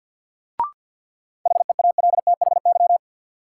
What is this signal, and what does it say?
Morse code, heard as a keyed tone, about 46 wpm, 700 Hz. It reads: HEALTHY
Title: Sentences from Top 1400 Words - New Word Drill Alt - 50wpm